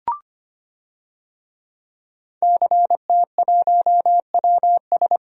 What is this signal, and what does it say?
Morse code, heard as a keyed tone, about 25 wpm, 700 Hz. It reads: CT1WS
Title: DXCC-100 Call Signs - 5 Characters - Repeated 2x - 25wpm